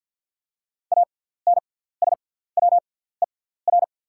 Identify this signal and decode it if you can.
Morse code, heard as a keyed tone, about 48 wpm, spaced as for 17 wpm, 700 Hz. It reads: ANSWER
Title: 6x Character Spacing - Top 300 Words - 50wpm